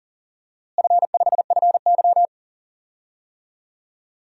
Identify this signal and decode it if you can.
Morse code, heard as a keyed tone, about 40 wpm, 700 Hz. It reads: F5FY
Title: DXCC-100 Call Signs - 4 Characters - Repeated 2x - 40wpm